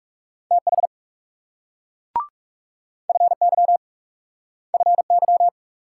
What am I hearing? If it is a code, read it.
Morse code, heard as a keyed tone, about 41 wpm, 700 Hz. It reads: TH FY FY